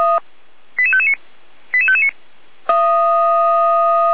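Five-tone selective call (ZVEI), heard as quick runs of five tones, tones of about 70 ms. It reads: 88559 88559
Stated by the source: ZVEI five-tone selective call received on an SDR.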